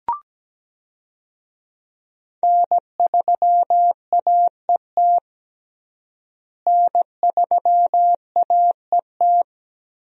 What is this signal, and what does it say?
Morse code, heard as a keyed tone, about 17 wpm, 700 Hz. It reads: N3AET N3AET
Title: USA Call Signs - 5 Characters - Repeated 2x - 17wpm